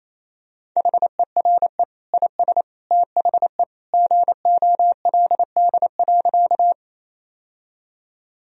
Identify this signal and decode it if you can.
Morse code, heard as a keyed tone, about 28 wpm, 700 Hz. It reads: HERE IS THE GOLD.
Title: Sentences from Top 1000 Words - Review Part 3 - 28wpm